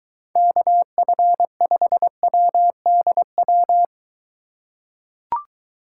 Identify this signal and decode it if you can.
Morse code, heard as a keyed tone, about 23 wpm, 700 Hz. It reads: KF5WDW